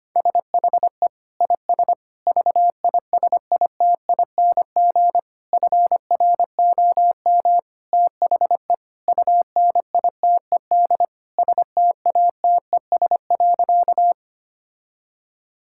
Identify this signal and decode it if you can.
Morse code, heard as a keyed tone, about 25 wpm, 700 Hz. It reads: SHE IS VISITING FROM THE UNITED STATES.
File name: practice / 25wpm / Sentences with 4-Word Phrases - 25wpm